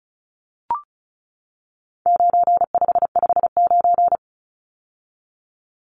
Morse code, heard as a keyed tone, about 34 wpm, 700 Hz. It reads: 9559
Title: Numbers - 4-Digit 35wpm